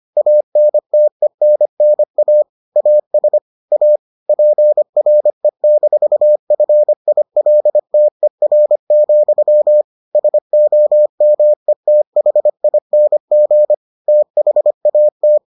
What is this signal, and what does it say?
Morse code, heard as a keyed tone, about 25 wpm, 600 Hz. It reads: ANTENNA AS A PRE-FILTER, SOMETHING THAT